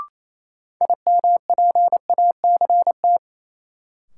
Morse code, heard as a keyed tone, about 27 wpm, 700 Hz. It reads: IMPACT
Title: Sentences from Top 1000 Words - Review Part 6 - New Word Drill - 28wpm